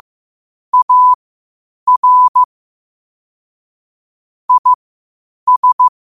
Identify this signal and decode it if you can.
Morse code, heard as a keyed tone, about 15 wpm, spaced as for 8 wpm, 1000 Hz. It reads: AR IS
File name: qotdfiles / 8wpm